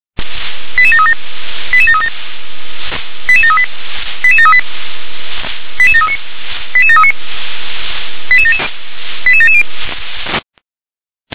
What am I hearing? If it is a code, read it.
Five-tone selective call (ZVEI), heard as quick runs of five tones, tones of about 70 ms. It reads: 88737 88737 88738 88738 88739 88739 88770